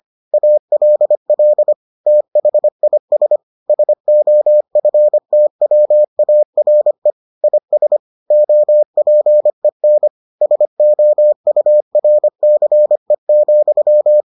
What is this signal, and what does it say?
Morse code, heard as a keyed tone, about 25 wpm, 600 Hz. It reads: ALL THIS SOFTWARE IS OPEN SOURCE,